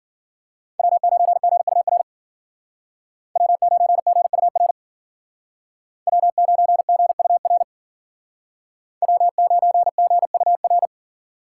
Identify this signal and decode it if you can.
Morse code, heard as a keyed tone, about 48 wpm, 700 Hz. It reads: W9GUR W9GUR W9GUR W9GUR
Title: Speed Racing - USA 1x3 Call Signs 40wpm